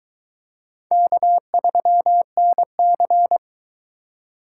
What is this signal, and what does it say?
Morse code, heard as a keyed tone, about 22 wpm, 700 Hz. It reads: K3NC